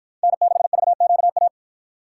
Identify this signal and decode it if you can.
Morse code, heard as a keyed tone, about 47 wpm, 700 Hz. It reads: N6VQA